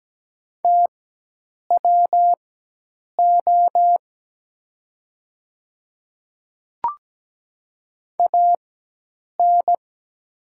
Morse code, heard as a keyed tone, about 17 wpm, spaced as for 8 wpm, 700 Hz. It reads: TWO AN
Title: 4x Character Spacing - Top 500 Words - 17wpm